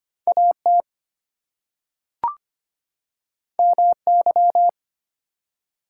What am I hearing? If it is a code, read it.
Morse code, heard as a keyed tone, about 25 wpm, 700 Hz. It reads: AT MY
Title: Two Letter Words - 25wpm